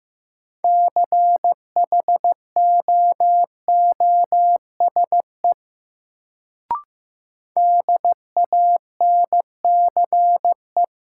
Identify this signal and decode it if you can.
Morse code, heard as a keyed tone, about 15 wpm, 700 Hz. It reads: CHOOSE DANCE
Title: Sentences from Top 700 Words - New Word Drill - 15wpm